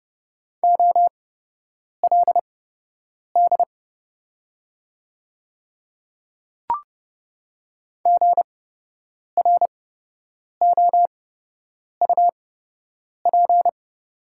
Morse code, heard as a keyed tone, about 30 wpm, spaced as for 8 wpm, 700 Hz. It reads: OLD GROUP
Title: 8x Character Spacing - Top 300 Words - 30wpm